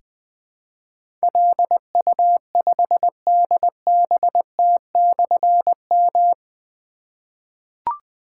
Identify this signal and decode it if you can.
Morse code, heard as a keyed tone, about 20 wpm, 700 Hz. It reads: LU5DBT/M
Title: Difficult Call Signs 20wpm